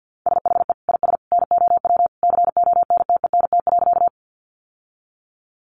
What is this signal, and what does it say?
Morse code, heard as a keyed tone, about 50 wpm, 700 Hz. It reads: SHE IS NOW CONTENT.
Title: Sentences from Top 900 Words 50wpm